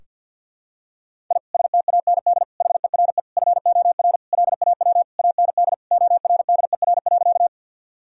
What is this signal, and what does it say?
Morse code, heard as a keyed tone, about 51 wpm, 700 Hz. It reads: I STAND HERE FOR LAW AND ORDER.
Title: Sentences from Top 1000 Words - Review Part 3 - 50wpm